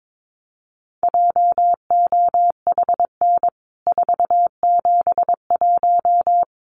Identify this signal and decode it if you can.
Morse code, heard as a keyed tone, about 22 wpm, 700 Hz. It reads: JOHN 471
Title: CWT Exchange 22wpm